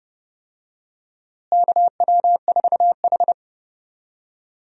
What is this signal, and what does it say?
Morse code, heard as a keyed tone, about 30 wpm, 700 Hz. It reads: KW4H
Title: USA Call Signs - 4 Characters - Repeated 3x - 30wpm